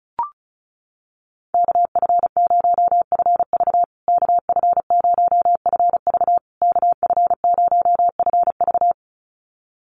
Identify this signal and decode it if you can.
Morse code, heard as a keyed tone, about 34 wpm, 700 Hz. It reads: KF0FV KF0FV KF0FV
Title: USA Call Signs Repeated 3x 35wpm